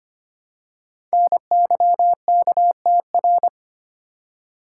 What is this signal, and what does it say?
Morse code, heard as a keyed tone, about 25 wpm, 700 Hz. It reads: NYKTR
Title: Code Groups - 5 Letters - 25wpm